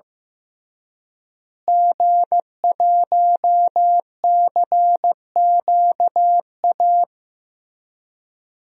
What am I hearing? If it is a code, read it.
Morse code, heard as a keyed tone, about 15 wpm, 700 Hz. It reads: G1CQA
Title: DXCC-100 Call Signs - 5 Characters - Repeated 3x - 15wpm